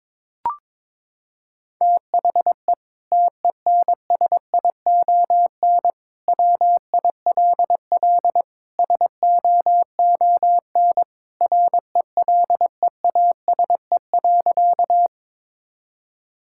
Morse code, heard as a keyed tone, about 22 wpm, 700 Hz. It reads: THE TENSION WILL SOON RELEASE.
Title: Sentences from Top 1600 Words 22wpm